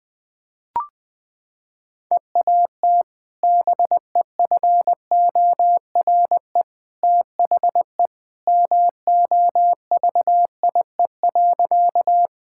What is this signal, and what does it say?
Morse code, heard as a keyed tone, about 20 wpm, 700 Hz. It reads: EAT BEFORE THE MOVIE.
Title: Sentences with 2-Word Phrases - 20wpm